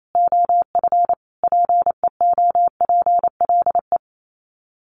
Morse code, heard as a keyed tone, about 28 wpm, 700 Hz. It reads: OF PEOPLE
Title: Sets of 5 Words Encore2 28wpm